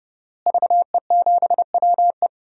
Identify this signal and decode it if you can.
Morse code, heard as a keyed tone, about 30 wpm, 700 Hz. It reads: VE7WE
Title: DXCC-100 Call Signs - 5 Characters - Repeated 3x - 30wpm